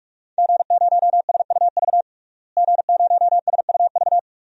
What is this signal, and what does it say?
Morse code, heard as a keyed tone, about 46 wpm, 700 Hz. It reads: G0SUV G0SUV
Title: DXCC-100 Call Signs - 5 Characters - Repeated 2x - 45wpm